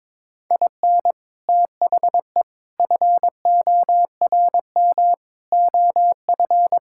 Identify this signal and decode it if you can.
Morse code, heard as a keyed tone, about 21 wpm, 700 Hz. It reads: IN THE FORM OF
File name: practice / 22wpm / Sentences with 4-Word Phrases - 22wpm